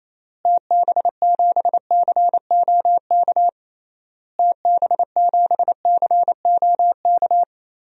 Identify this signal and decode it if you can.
Morse code, heard as a keyed tone, about 28 wpm, 700 Hz. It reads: TB7COK TB7COK
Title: DXCC-100 Call Signs - 6 Characters - Repeated 3x - 28wpm